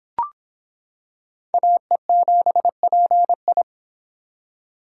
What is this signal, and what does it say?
Morse code, heard as a keyed tone, about 26 wpm, 700 Hz. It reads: AE7PI